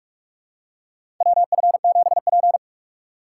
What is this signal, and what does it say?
Morse code, heard as a keyed tone, about 41 wpm, 700 Hz. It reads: WF7P
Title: USA Call Signs - 4 Characters - 4x Character Spacing - 45wpm